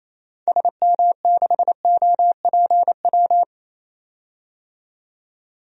Morse code, heard as a keyed tone, about 28 wpm, 700 Hz. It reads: SM6OPW
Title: DXCC-100 Call Signs - 6 Characters - 28wpm